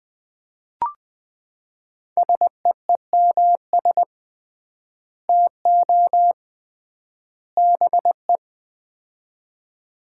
Morse code, heard as a keyed tone, about 21 wpm, 700 Hz. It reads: SEEMS TO BE
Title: 3x Word-Spacing - Sets of 3 Words - 20wpm